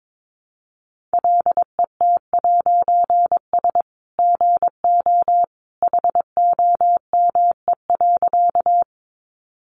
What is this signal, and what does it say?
Morse code, heard as a keyed tone, about 22 wpm, 700 Hz. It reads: LET'S GO HOME.
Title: A Sound Of Thunder 22wpm